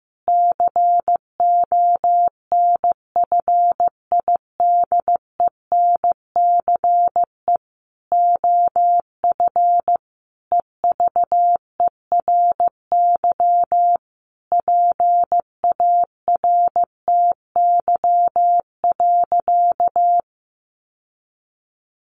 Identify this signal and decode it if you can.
Morse code, heard as a keyed tone, about 15 wpm, 700 Hz. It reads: CONFIDENCE OF EVERY PARTY.
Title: Sentences from Top 1500 Words 15wpm